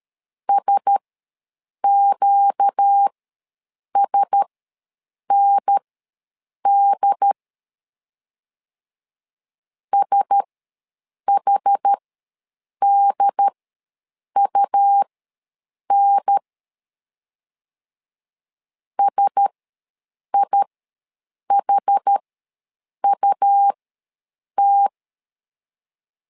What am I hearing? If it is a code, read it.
Morse code, heard as a keyed tone, about 13 wpm, spaced as for 7 wpm, 800 Hz. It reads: SQSND SHDUN SIHUT
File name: LEZIONI / 4 Q1S ws 12wpm 1827